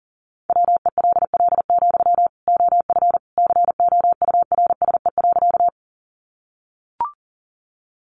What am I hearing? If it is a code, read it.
Morse code, heard as a keyed tone, about 40 wpm, 700 Hz. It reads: WELL, OF COURSE.
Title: Sentences from Top 500 Words 40wpm